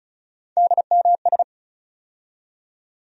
Morse code, heard as a keyed tone, about 36 wpm, 700 Hz. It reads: DMS
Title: USA Call Sign Postfixes - 3 Letters 35wpm